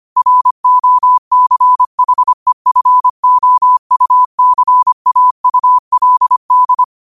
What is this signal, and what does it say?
Morse code, heard as a keyed tone, about 25 wpm, 1000 Hz. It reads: ROCHEFOUCAULD